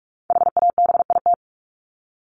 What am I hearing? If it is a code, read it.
Morse code, heard as a keyed tone, about 44 wpm, 700 Hz. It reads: HABIT
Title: Top 1751-2000 Words 45wpm